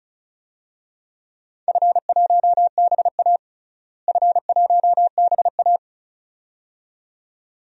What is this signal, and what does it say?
Morse code, heard as a keyed tone, about 34 wpm, 700 Hz. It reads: F1BA F1BA
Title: DXCC-100 Call Signs - 4 Characters - Repeated 2x - 35wpm